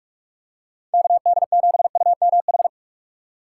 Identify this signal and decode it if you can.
Morse code, heard as a keyed tone, about 45 wpm, 700 Hz. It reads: KD7UMH
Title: USA Call Signs - 6 Characters - 2x Character Spacing - 45wpm